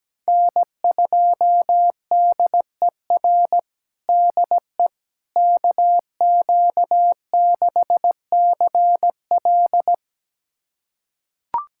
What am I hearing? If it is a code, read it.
Morse code, heard as a keyed tone, about 17 wpm, 700 Hz. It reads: N2DER DE KQ6CL
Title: USA Callsign DE Callsign - 5 Characters - 17wpm